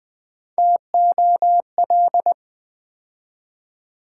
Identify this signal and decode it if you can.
Morse code, heard as a keyed tone, about 20 wpm, 700 Hz. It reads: TOL
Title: Build A Word - 3 Letters At A Time - 9-Letter Words - 20wpm